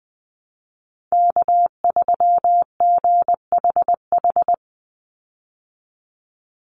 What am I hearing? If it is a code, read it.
Morse code, heard as a keyed tone, about 20 wpm, 700 Hz. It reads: K3GHH
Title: Call Signs Repeated 3x - 20wpm